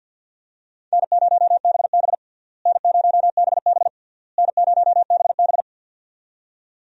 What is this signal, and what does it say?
Morse code, heard as a keyed tone, about 46 wpm, 700 Hz. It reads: N0BB N0BB N0BB
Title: USA Call Signs - 4 Characters - Repeated 3x - 50wpm